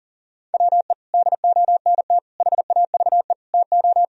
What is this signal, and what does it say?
Morse code, heard as a keyed tone, about 40 wpm, 700 Hz. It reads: WE DONT HAVE TO
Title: Sentences from Top 1000 Words - Review Part 2 - 40wpm